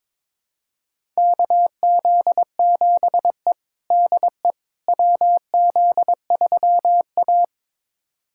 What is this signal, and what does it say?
Morse code, heard as a keyed tone, about 22 wpm, 700 Hz. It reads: KZ7E DE WZ3A
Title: USA Callsign DE Callsign - 4 Characters - 22wpm